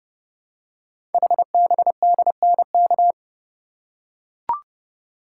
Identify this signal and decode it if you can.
Morse code, heard as a keyed tone, about 30 wpm, 700 Hz. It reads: HBDNK
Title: Code Groups - 5 Letters - 30wpm